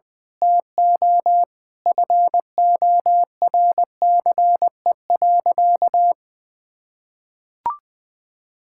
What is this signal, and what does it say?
Morse code, heard as a keyed tone, about 20 wpm, 700 Hz. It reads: TO FORCE.
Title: Sentences from Top 1000 Words - Review Part 6 - 20wpm